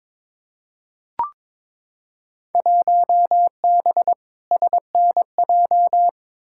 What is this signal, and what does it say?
Morse code, heard as a keyed tone, about 21 wpm, 700 Hz. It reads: 1B SNJ
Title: Field Day Exchanges - 22wpm